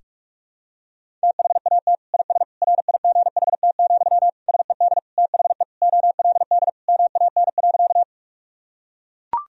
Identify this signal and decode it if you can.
Morse code, heard as a keyed tone, about 41 wpm, 700 Hz. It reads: THAT IS RIGHT, SED THE OLD MAN.
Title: Sentences from Top 200 Words and Abbreviations 45wpm